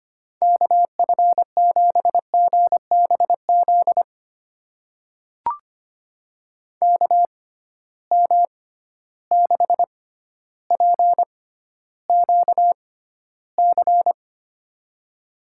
Morse code, heard as a keyed tone, about 25 wpm, spaced as for 13 wpm, 700 Hz. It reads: KF7GBZ KM6PQC
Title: USA Call Signs - 6 Characters - 6x Character Spacing - 25wpm